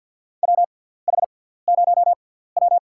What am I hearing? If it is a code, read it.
Morse code, heard as a keyed tone, about 52 wpm, spaced as for 17 wpm, 700 Hz. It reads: WH0W
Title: USA Call Signs - 4 Characters - 6x Character Spacing - 50wpm